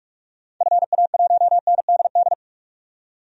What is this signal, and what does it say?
Morse code, heard as a keyed tone, about 46 wpm, 700 Hz. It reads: FA1NDD
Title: DXCC-100 Call Signs - 6 Characters - 45wpm